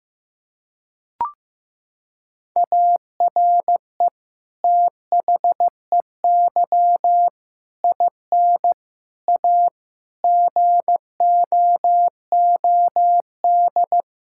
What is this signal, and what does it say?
Morse code, heard as a keyed tone, about 15 wpm, 700 Hz. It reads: ARE THEY IN A GOOD
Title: Sentences from Top 1000 Words - Review Part 6 - 15wpm